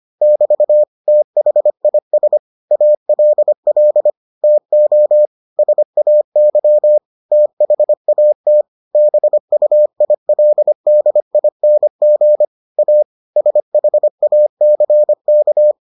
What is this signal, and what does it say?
Morse code, heard as a keyed tone, about 25 wpm, 600 Hz. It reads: <BT> THIS ALL TO SAY THAT BUILDING A SHACK